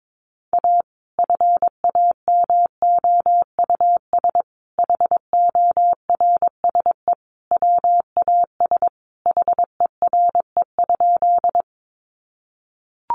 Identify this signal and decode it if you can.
Morse code, heard as a keyed tone, about 22 wpm, 700 Hz. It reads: A FAMOUS HORSE WAS HERE?
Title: Sentences from Top 1500 Words 22wpm